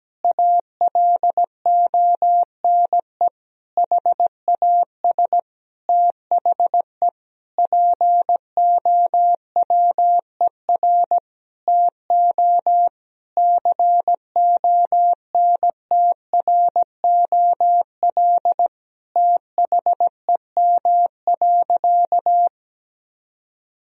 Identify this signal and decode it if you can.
Morse code, heard as a keyed tone, about 17 wpm, 700 Hz. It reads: ALONE HAS THE POWER TO CONTROL THEM.
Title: Sentences from Top 1000 Words - Review Part 2 - 17wpm